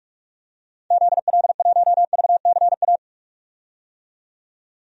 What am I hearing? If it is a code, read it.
Morse code, heard as a keyed tone, about 46 wpm, 700 Hz. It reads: ZL1VCA
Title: DXCC-100 Call Signs - 6 Characters - 45wpm